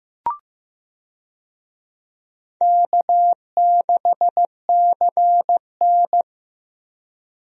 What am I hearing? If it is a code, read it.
Morse code, heard as a keyed tone, about 15 wpm, 700 Hz. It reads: K6CN
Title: USA Call Signs - 4 Characters - Repeated 3x - 15wpm